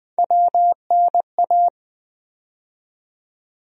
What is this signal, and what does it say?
Morse code, heard as a keyed tone, about 20 wpm, 700 Hz. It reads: WNA
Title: USA Call Sign Postfixes - 3 Letters 20wpm